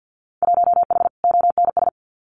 Morse code, heard as a keyed tone, about 50 wpm, 700 Hz. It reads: 1H ONS
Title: Winter Field Day Exchanges 50wpm